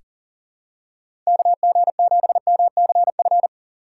Morse code, heard as a keyed tone, about 37 wpm, 700 Hz. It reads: KG7MCF
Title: USA Call Signs - 6 Characters - 6x Character Spacing - 40wpm